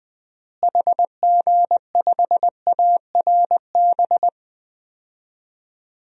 Morse code, heard as a keyed tone, about 20 wpm, 700 Hz. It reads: HG5ARB
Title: DXCC-100 Call Signs - 6 Characters - Repeated 2x - 20wpm